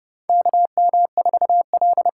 Morse code, heard as a keyed tone, about 30 wpm, 700 Hz. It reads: KM4L
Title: USA Call Signs - 6 Characters - 6x Character Spacing - 30wpm